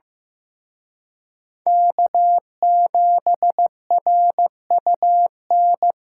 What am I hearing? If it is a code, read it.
Morse code, heard as a keyed tone, about 18 wpm, 700 Hz. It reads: K7RUN
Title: Speed Racing - USA 1x3 Call Signs 15wpm